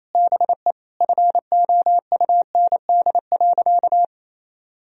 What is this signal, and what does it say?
Morse code, heard as a keyed tone, about 27 wpm, 700 Hz. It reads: BE FOUND.
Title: Sentences from Top 1000 Words - Review Part 3 - 28wpm